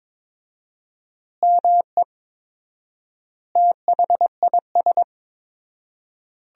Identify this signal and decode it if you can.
Morse code, heard as a keyed tone, about 22 wpm, 700 Hz. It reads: ME THIS
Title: 4x Word-Spacing - Sentences from Top 500 Words - 22wpm